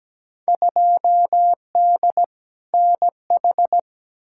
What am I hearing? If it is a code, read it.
Morse code, heard as a keyed tone, about 17 wpm, 700 Hz. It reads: 2D NH
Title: Field Day Exchanges Encore - 17wpm